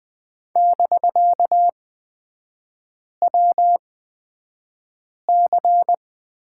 Morse code, heard as a keyed tone, about 20 wpm, spaced as for 5 wpm, 700 Hz. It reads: <BK>WC
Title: 250 Lesson 40e - BK - Three Characters 20wpm